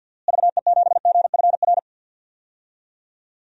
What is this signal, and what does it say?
Morse code, heard as a keyed tone, about 48 wpm, 700 Hz. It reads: VE7GFR
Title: DXCC-100 Call Signs - 6 Characters - Repeated 3x - 50wpm